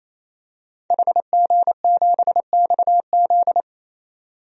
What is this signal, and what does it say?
Morse code, heard as a keyed tone, about 27 wpm, 700 Hz. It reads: HG7XZ
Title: DXCC-100 Call Signs - 5 Characters - 28wpm